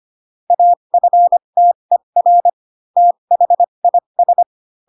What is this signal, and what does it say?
Morse code, heard as a keyed tone, about 25 wpm, spaced as for 22 wpm, 700 Hz. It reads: AFTER THIS
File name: Pinocchio / Pinocchio ch2 18 25 ik1hge